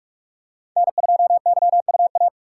Morse code, heard as a keyed tone, about 47 wpm, 700 Hz. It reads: N2YUA
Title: USA Call Signs - 5 Characters - 3x Character Spacing - 45wpm